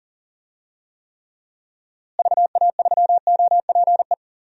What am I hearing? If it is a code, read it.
Morse code, heard as a keyed tone, about 40 wpm, 700 Hz. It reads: VA3OPE
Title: DXCC-100 Call Signs - 6 Characters - Repeated 2x - 40wpm